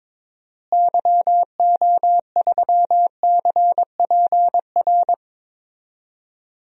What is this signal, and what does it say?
Morse code, heard as a keyed tone, about 22 wpm, 700 Hz. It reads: YO3CPR
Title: DXCC-100 Call Signs - 6 Characters - Repeated 2x - 22wpm